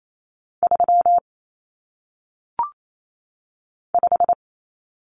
Morse code, heard as a keyed tone, about 28 wpm, 700 Hz. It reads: 3 5
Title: Numbers - 1-Digit 28wpm